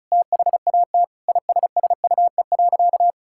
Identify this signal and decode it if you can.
Morse code, heard as a keyed tone, about 34 wpm, 700 Hz. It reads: THAT ISSUE.
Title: Sentences from Top 1000 Words - Review Part 6 - 35wpm